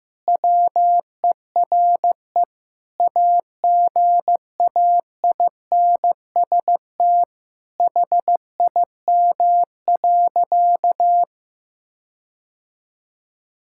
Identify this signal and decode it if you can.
Morse code, heard as a keyed tone, about 15 wpm, 700 Hz. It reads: WERE AGAINST HIM.
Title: Sentences from Top 1600 Words 15wpm